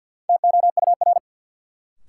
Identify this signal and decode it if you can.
Morse code, heard as a keyed tone, about 53 wpm, 700 Hz. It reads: TOUR
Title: Sentences from Top 1400 Words - New Word Drill Alt - 50wpm